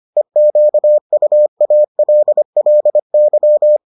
Morse code, heard as a keyed tone, about 25 wpm, 600 Hz. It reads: EQUALLY